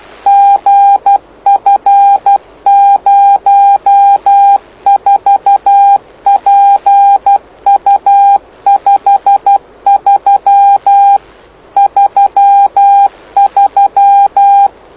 Morse code, heard as a keyed tone, about 12 wpm, 800 Hz. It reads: GF04PU53 33
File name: lu4aao / lu7aa 09dic2011 exp planeador vuelo1